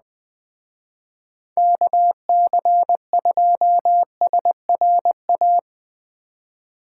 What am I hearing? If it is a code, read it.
Morse code, heard as a keyed tone, about 20 wpm, 700 Hz. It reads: KC2SRA
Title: Speed Racing - USA 2x3 Call Signs 17wpm